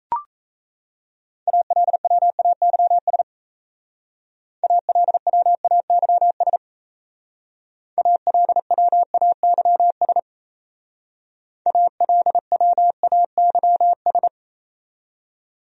Morse code, heard as a keyed tone, about 37 wpm, 700 Hz. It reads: ALWAYS ALWAYS ALWAYS ALWAYS